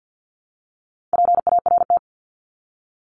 Morse code, heard as a keyed tone, about 50 wpm, 700 Hz. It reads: PART
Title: ES:S Suffix - Top 100 Words with Base Words 50wpm